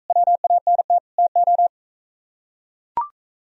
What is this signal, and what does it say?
Morse code, heard as a keyed tone, about 43 wpm, 700 Hz. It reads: WANT TO